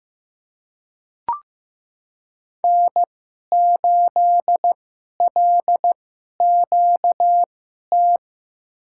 Morse code, heard as a keyed tone, about 15 wpm, spaced as for 11 wpm, 700 Hz. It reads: N8LQT
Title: USA Call Signs - 5 Characters - 2x Character Spacing - 15wpm